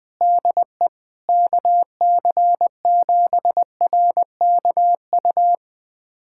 Morse code, heard as a keyed tone, about 20 wpm, 700 Hz. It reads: DE KC7RKU
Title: USA Callsign DE Callsign 20wpm